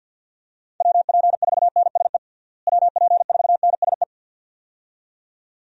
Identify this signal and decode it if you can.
Morse code, heard as a keyed tone, about 52 wpm, 700 Hz. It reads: WP4NSE WP4NSE
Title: USA Call Signs - 6 Characters - Repeated 2x - 50wpm